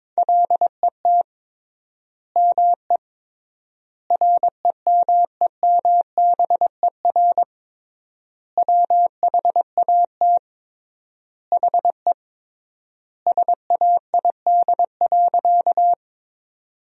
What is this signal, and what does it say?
Morse code, heard as a keyed tone, about 22 wpm, 700 Hz. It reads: LET ME REMEMBER WHAT HE SAID.
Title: 3x Word-Spacing - Sentences from Top 400 Words - 22wpm